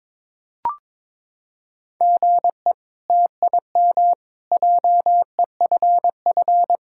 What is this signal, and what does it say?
Morse code, heard as a keyed tone, about 22 wpm, 700 Hz. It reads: GE TIM JEFF
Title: K1USN SST QSO Practice - 22wpm